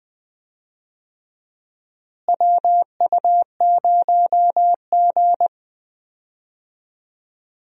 Morse code, heard as a keyed tone, about 20 wpm, 700 Hz. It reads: WU0G
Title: USA Call Signs - 4 Characters - Repeated 2x - 20wpm